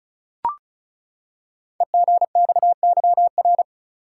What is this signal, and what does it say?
Morse code, heard as a keyed tone, about 36 wpm, 690 Hz. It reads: EGXYR
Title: Code Groups - 5 Letters - 35wpm